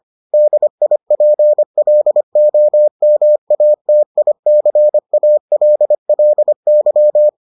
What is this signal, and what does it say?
Morse code, heard as a keyed tone, about 25 wpm, 600 Hz. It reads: DIPLOMATICALLY